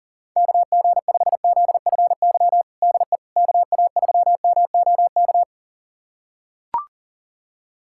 Morse code, heard as a keyed tone, about 40 wpm, 700 Hz. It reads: KG5ZFY DE KA3MOK
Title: USA Callsign DE Callsign - 6 Characters - 40wpm